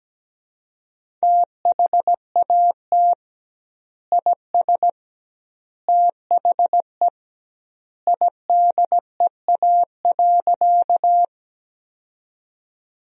Morse code, heard as a keyed tone, about 17 wpm, 700 Hz. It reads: THAT IS THE IDEA.
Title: 2x Word-Spacing - Sentences from Top 300 Words - 17wpm